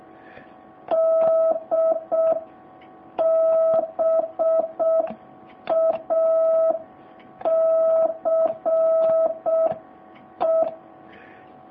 Morse code, heard as a keyed tone, about 6 wpm, 650 Hz. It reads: DBACE